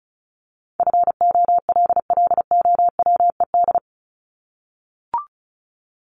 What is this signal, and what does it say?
Morse code, heard as a keyed tone, about 34 wpm, 700 Hz. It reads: FOLLOWED
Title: ED Suffix - Top 100 Words with Base Words 35wpm